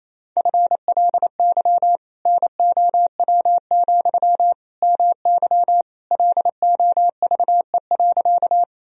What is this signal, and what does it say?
Morse code, heard as a keyed tone, about 28 wpm, 700 Hz. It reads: FLY NOW, MY LOVE.
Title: Sentences from Top 1000 Words - Review Part 2 - 28wpm